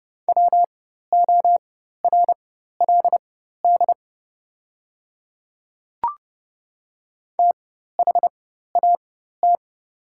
Morse code, heard as a keyed tone, about 30 wpm, spaced as for 14 wpm, 700 Hz. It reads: WORLD THAT
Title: 4x Character Spacing - Top 200 Words - 30wpm